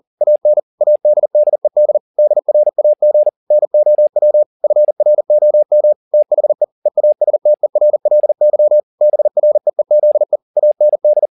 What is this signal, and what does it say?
Morse code, heard as a keyed tone, about 40 wpm, 600 Hz. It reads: AN ADDED DRAG NOW FROM THE EASTERLY BREEZE AND